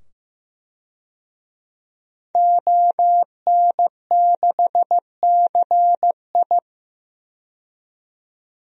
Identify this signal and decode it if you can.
Morse code, heard as a keyed tone, about 15 wpm, 700 Hz. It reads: ON6CI
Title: DXCC-100 Call Signs - 5 Characters - 15wpm